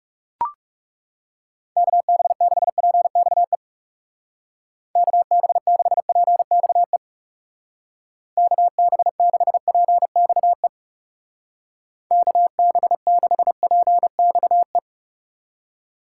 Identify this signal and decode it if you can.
Morse code, heard as a keyed tone, about 36 wpm, 700 Hz. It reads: KB6PXE KB6PXE KB6PXE KB6PXE
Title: Speed Racing - USA Call Signs 30wpm